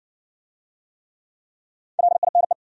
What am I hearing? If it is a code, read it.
Morse code, heard as a keyed tone, about 58 wpm, 700 Hz. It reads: LINE